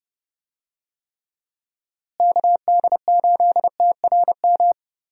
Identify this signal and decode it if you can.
Morse code, heard as a keyed tone, about 30 wpm, 700 Hz. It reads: KD8TRM
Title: USA Call Signs - 6 Characters - Repeated 2x - 30wpm